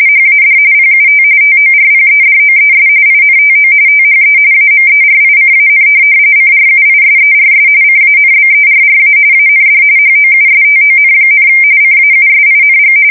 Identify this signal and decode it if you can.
RTTY, about 45.45 baud. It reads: HAD PUT TORPEDOES INTO THE AMERICAN CRUISERS CANBERRA AND HOUSTON, CAUSING
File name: fsk170 / USS-New-Jersey